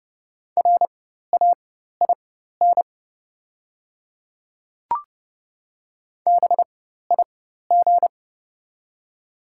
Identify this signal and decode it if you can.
Morse code, heard as a keyed tone, about 30 wpm, spaced as for 14 wpm, 690 Hz. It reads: RAIN BIG
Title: 4x Character Spacing - Top 500 Words - 30wpm